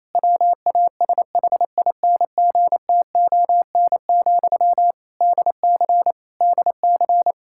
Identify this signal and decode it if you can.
Morse code, heard as a keyed tone, about 27 wpm, 700 Hz. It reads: WASHINGTON, DC DC
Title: Practice QSO - Exchange 1 - Beginning - 28wpm